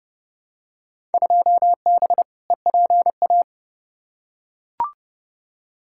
Morse code, heard as a keyed tone, about 30 wpm, 700 Hz. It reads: 2B EPA
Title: Field Day Exchanges Encore - 30wpm